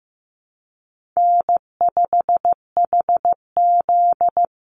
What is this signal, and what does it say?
Morse code, heard as a keyed tone, about 15 wpm, 700 Hz. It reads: N5HZ
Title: Easy Call Signs - 15wpm